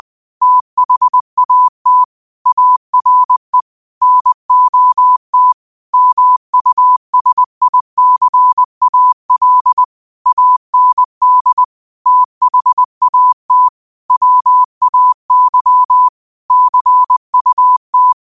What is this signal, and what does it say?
Morse code, heard as a keyed tone, about 20 wpm, 1000 Hz. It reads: THAT ARE NOT MUSICAL AND THAT WAY CUT
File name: qotdfiles / 20wpm